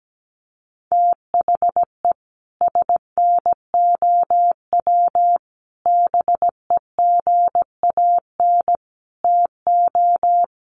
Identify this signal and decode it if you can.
Morse code, heard as a keyed tone, about 17 wpm, 700 Hz. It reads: THE SNOW BEGAN TO
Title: Sentences from Top 1500 Words 17wpm